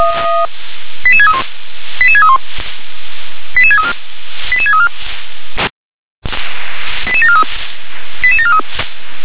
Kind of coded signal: five-tone selective call